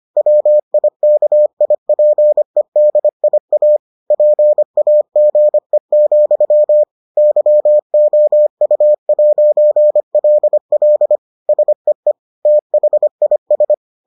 Morse code, heard as a keyed tone, about 25 wpm, 600 Hz. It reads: WIKIPEDIA PAGE, YOU'LL SEE THIS